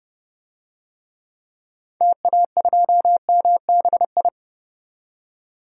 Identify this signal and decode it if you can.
Morse code, heard as a keyed tone, about 30 wpm, 700 Hz. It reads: TA2MBI
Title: DXCC-100 Call Signs - 6 Characters - 30wpm